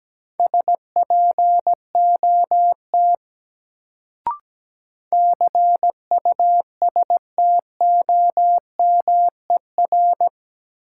Morse code, heard as a keyed tone, about 17 wpm, 700 Hz. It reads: SPOT CUSTOMER
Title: Sentences from Top 1100 Words - New Word Drill - 17wpm